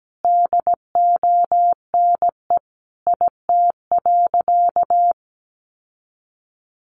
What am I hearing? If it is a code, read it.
Morse code, heard as a keyed tone, about 17 wpm, 700 Hz. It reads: DONE IT.
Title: Sentences from Top 1200 Words 17wpm